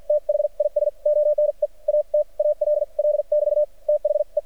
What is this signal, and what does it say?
Morse code, heard as a keyed tone, about 53 wpm, spaced as for 43 wpm, 600 Hz. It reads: THIS ONE AT ALL <BT> THE
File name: archived / CRSnet